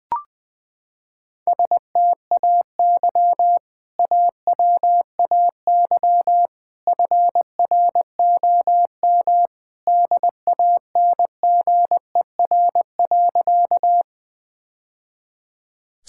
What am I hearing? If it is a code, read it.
Morse code, heard as a keyed tone, about 21 wpm, 700 Hz. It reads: STAY AWAY FROM DANGER.
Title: Sentences with 2-Word Phrases - 20wpm